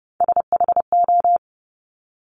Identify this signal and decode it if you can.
Morse code, heard as a keyed tone, about 30 wpm, 690 Hz. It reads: SHO
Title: Top Three Letter Combinations in Words 30wpm